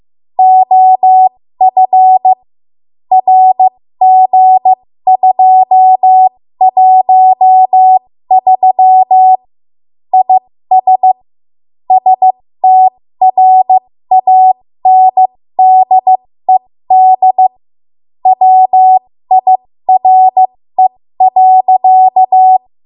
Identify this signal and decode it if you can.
Morse code, heard as a keyed tone, about 15 wpm, spaced as for 13 wpm, 750 Hz. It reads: OF RG213 IS STRANDED WIRE.
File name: Morse / 13WPM